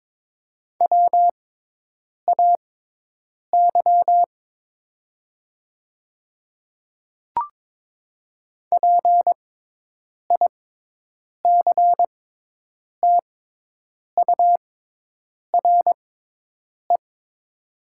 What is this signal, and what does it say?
Morse code, heard as a keyed tone, about 22 wpm, spaced as for 8 wpm, 700 Hz. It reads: WAY PICTURE
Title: 6x Character Spacing - Top 200 Words - 22wpm